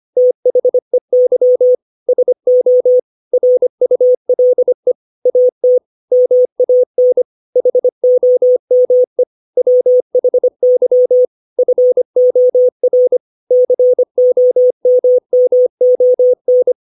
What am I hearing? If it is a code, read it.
Morse code, heard as a keyed tone, about 25 wpm, 500 Hz. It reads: THEY SO RULE AT MAN HOME WHY FOR COMMON
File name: 632 words / 632 Words 25WPM